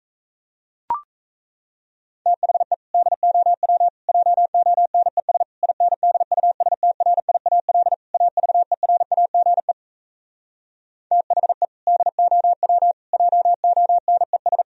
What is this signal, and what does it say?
Morse code, heard as a keyed tone, about 37 wpm, 700 Hz. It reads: THE DOW JONES INDUSTRIAL AVERAGE THE DOW JONES